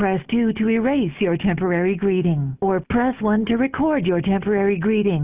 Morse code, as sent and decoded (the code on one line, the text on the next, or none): none